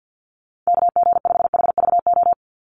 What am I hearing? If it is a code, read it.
Morse code, heard as a keyed tone, about 50 wpm, 700 Hz. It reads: KG5HVO
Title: Call Signs - 50wpm